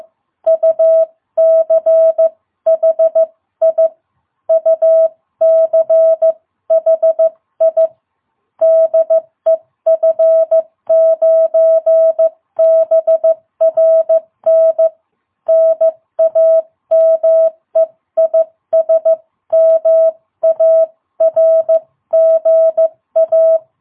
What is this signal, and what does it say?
Morse code, heard as a keyed tone, about 14 wpm, 660 Hz. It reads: UCHI UCHI DE F9BRN NAME IS MARGA
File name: jh8cbh / 2022cw